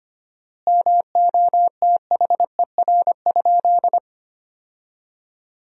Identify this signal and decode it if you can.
Morse code, heard as a keyed tone, about 25 wpm, 700 Hz. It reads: MOTHER?
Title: 4x Word-Spacing - Sentences from Top 200 Words - 25wpm